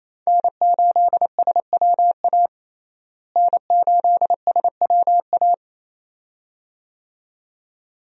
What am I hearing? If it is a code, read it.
Morse code, heard as a keyed tone, about 27 wpm, 700 Hz. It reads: N8SWA N8SWA
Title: USA Call Signs - 5 Characters - Repeated 2x - 28wpm